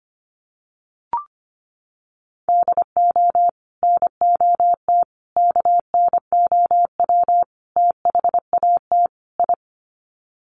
Morse code, heard as a keyed tone, about 25 wpm, 700 Hz. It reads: DO NOT KNOW THAT I
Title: Sets of 5 Words Encore2 25wpm